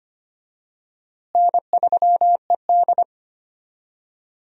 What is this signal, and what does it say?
Morse code, heard as a keyed tone, about 25 wpm, 700 Hz. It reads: N3ED
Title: USA Call Signs - 4 Characters - Repeated 3x - 25wpm